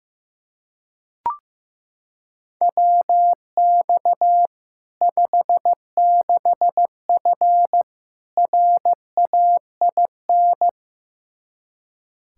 Morse code, heard as a keyed tone, about 15 wpm, 700 Hz. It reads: WX 56F RAIN
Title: Practice QSO - Exchange 2 - Info - 15wpm